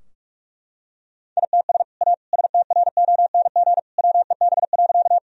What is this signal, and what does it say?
Morse code, heard as a keyed tone, about 41 wpm, 700 Hz. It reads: ITS A STRONG WEB.
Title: Sentences from Top 1000 Words - Review Part 4 - 45wpm